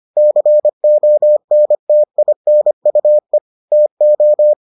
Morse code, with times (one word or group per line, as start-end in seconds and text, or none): 0.2-3.4 CONTINUE
3.7-4.5 TO